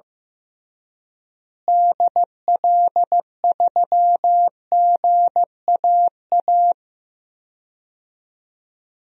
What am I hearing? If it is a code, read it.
Morse code, heard as a keyed tone, about 15 wpm, 700 Hz. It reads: DL3GAA
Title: DXCC-100 Call Signs - 6 Characters - Repeated 3x - 15wpm